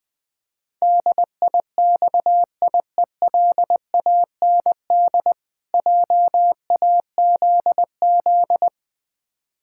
Morse code, heard as a keyed tone, about 20 wpm, 700 Hz. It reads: DIXIELAND JAZZ
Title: Difficult Words - Speed-Racing with Hints - 20wpm